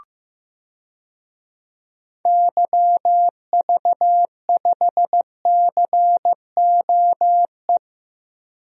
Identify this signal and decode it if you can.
Morse code, heard as a keyed tone, about 15 wpm, 700 Hz. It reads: YV5COE
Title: DXCC-100 Call Signs - 6 Characters - Repeated 3x - 15wpm